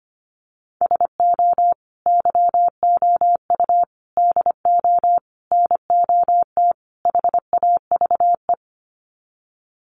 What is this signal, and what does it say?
Morse code, heard as a keyed tone, about 25 wpm, 700 Hz. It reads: SO YOU DO NOT HAVE
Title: Sets of 5 Words Encore 25wpm